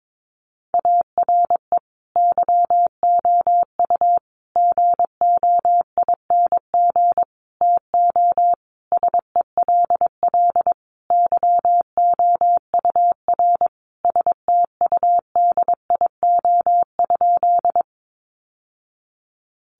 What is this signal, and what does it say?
Morse code, heard as a keyed tone, about 22 wpm, 700 Hz. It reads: ARE YOU GOING TO SELL YOUR STUDIO?
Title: Sentences from Top 1200 Words 22wpm